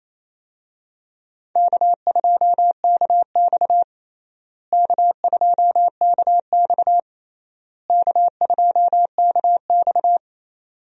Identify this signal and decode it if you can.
Morse code, heard as a keyed tone, about 28 wpm, 700 Hz. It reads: K2KX K2KX K2KX
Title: USA Call Signs - 4 Characters - Repeated 3x - 28wpm